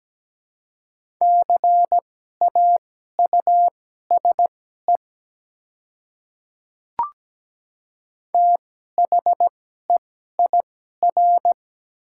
Morse code, heard as a keyed tone, about 17 wpm, spaced as for 12 wpm, 700 Hz. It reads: CAUSE THEIR
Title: 2x Character Spacing - Top 300 Words - 17wpm